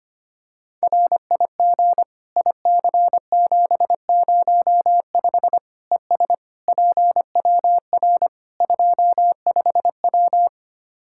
Morse code, heard as a keyed tone, about 25 wpm, 700 Hz. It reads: RIG IC705 ES PWR 25W
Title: Practice QSO - Exchange 2 - Info - 25wpm